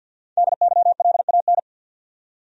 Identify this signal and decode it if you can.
Morse code, heard as a keyed tone, about 47 wpm, 700 Hz. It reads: DYLAN
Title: USA Common Names - 50wpm